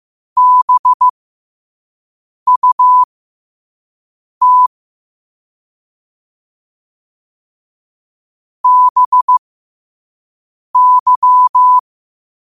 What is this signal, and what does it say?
Morse code, heard as a keyed tone, about 15 wpm, spaced as for 5 wpm, 1000 Hz. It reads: BUT BY
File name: qotdfiles / 5wpm